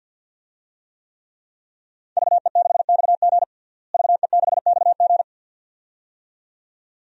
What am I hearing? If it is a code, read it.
Morse code, heard as a keyed tone, about 46 wpm, 700 Hz. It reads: VE6XG VE6XG
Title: DXCC-100 Call Signs - 5 Characters - Repeated 2x - 50wpm